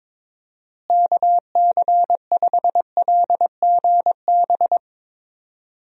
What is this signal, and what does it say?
Morse code, heard as a keyed tone, about 22 wpm, 700 Hz. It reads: KC5LGB
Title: USA Call Signs - 6 Characters - Repeated 3x - 22wpm